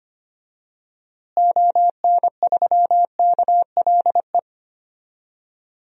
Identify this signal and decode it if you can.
Morse code, heard as a keyed tone, about 25 wpm, 700 Hz. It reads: ON3KLE
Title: DXCC-100 Call Signs - 6 Characters - Repeated 2x - 25wpm